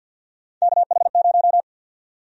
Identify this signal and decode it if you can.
Morse code, heard as a keyed tone, about 52 wpm, 700 Hz. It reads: KH0
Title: USA Call Sign Prefixes - 50wpm